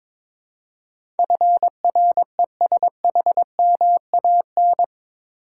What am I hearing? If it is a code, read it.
Morse code, heard as a keyed tone, about 22 wpm, 700 Hz. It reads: FRESHMAN
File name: practice / 22wpm / Eight Letter Words - 22wpm